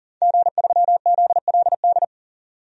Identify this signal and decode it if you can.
Morse code, heard as a keyed tone, about 40 wpm, 700 Hz. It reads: G3ZLD
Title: DXCC-100 Call Signs - 5 Characters - Repeated 3x - 40wpm